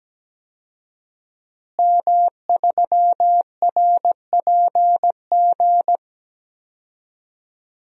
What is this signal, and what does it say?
Morse code, heard as a keyed tone, about 17 wpm, 700 Hz. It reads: M3RPG
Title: DXCC-100 Call Signs - 5 Characters - Repeated 2x - 17wpm